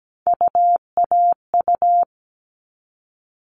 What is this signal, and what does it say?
Morse code, heard as a keyed tone, about 17 wpm, 700 Hz. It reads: UAU
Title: All Cut Numbers - 3-Digit 17wpm